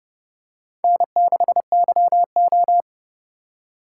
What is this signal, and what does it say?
Morse code, heard as a keyed tone, about 30 wpm, 700 Hz. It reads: N6YO
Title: USA Call Signs - 8x Character Spacing - 30wpm